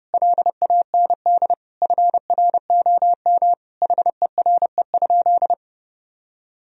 Morse code, heard as a keyed tone, about 30 wpm, 700 Hz. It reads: LAND FROM HERE?
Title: Sentences from Top 1000 Words - Review Part 1 - 30wpm